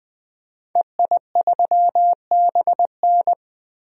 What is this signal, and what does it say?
Morse code, heard as a keyed tone, about 20 wpm, 700 Hz. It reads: EI3BN
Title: DXCC-100 Call Signs - 5 Characters - Repeated 3x - 20wpm